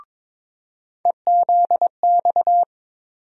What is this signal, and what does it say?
Morse code, heard as a keyed tone, about 22 wpm, 700 Hz. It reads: EZX
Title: USA Call Sign Postfixes - 3 Letters 22wpm